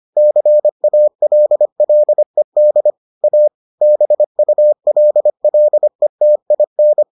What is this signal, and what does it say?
Morse code, heard as a keyed tone, about 25 wpm, 600 Hz. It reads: CALLED A BULLETIN